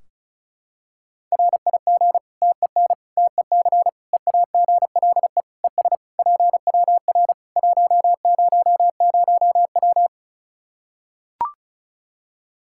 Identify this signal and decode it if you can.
Morse code, heard as a keyed tone, about 36 wpm, 700 Hz. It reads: RIG TEN TEC EAGLE ES PWR 100W
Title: Practice QSO - Exchange 2 - Info - 35wpm